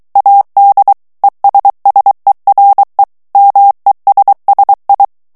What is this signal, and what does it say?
Morse code, heard as a keyed tone, about 23 wpm, 800 Hz. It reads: AD ESSERE MESSI
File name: ITALIEN texte clair 2 vitesse 22 mots minute